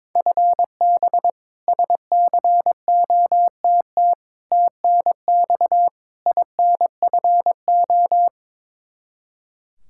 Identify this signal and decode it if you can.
Morse code, heard as a keyed tone, about 22 wpm, 700 Hz. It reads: FB SCOTT TNX INFO
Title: K1USN SST QSO Practice 22wpm